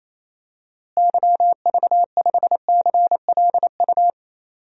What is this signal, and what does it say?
Morse code, heard as a keyed tone, about 28 wpm, 700 Hz. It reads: YV5CLU
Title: DXCC-100 Call Signs - 6 Characters - Repeated 2x - 28wpm